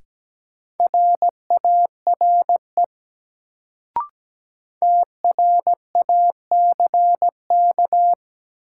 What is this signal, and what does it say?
Morse code, heard as a keyed tone, about 17 wpm, 700 Hz. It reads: RARE TRACK
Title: Sentences from Top 1100 Words - New Word Drill - 17wpm